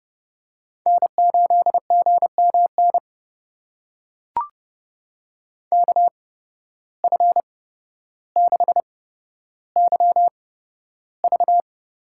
Morse code, heard as a keyed tone, about 30 wpm, spaced as for 13 wpm, 700 Hz. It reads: N8GMN KF6YV
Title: USA Call Signs - 5 Characters - 8x Character Spacing - 30wpm